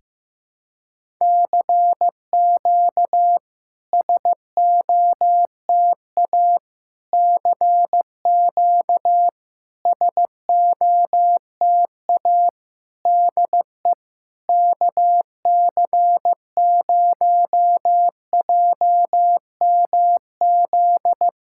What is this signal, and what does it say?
Morse code, heard as a keyed tone, about 15 wpm, 700 Hz. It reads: CQ SOTA CQ SOTA DE KC0JMZ
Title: SOTA QSO Practice 15wpm